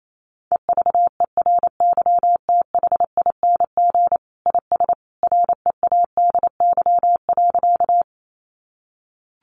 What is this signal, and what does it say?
Morse code, heard as a keyed tone, about 28 wpm, 700 Hz. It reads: EVERYTHING IS READY.
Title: Sentences from Top 700 Words 28wpm